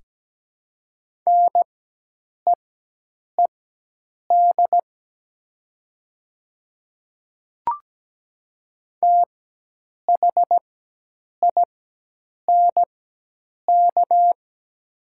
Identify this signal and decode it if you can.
Morse code, heard as a keyed tone, about 17 wpm, spaced as for 8 wpm, 700 Hz. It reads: NEED THINK
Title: 4x Character Spacing - Top 200 Words - 17wpm